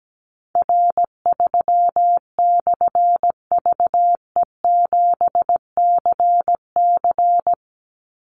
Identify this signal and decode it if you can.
Morse code, heard as a keyed tone, about 17 wpm, 700 Hz. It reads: R3/VE7CC
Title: Call Signs - 17wpm